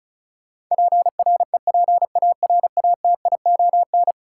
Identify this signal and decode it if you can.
Morse code, heard as a keyed tone, about 36 wpm, 700 Hz. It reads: PREPARATION
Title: Sentences from Top 1400 Words - New Word Drill - 35wpm